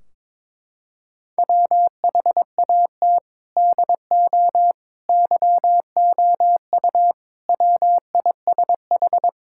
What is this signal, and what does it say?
Morse code, heard as a keyed tone, about 22 wpm, 700 Hz. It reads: WHAT DO YOU WISH
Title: Sentences from Top 1600 Words 22wpm